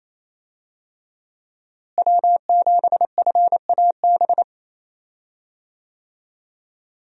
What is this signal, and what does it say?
Morse code, heard as a keyed tone, about 28 wpm, 700 Hz. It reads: W7FAB
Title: USA Call Signs - 5 Characters - Repeated 3x - 28wpm